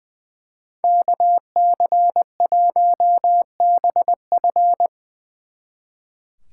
Morse code, heard as a keyed tone, about 22 wpm, 700 Hz. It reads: KC1BF
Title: Speed Racing - USA 2x2 Call Signs 20wpm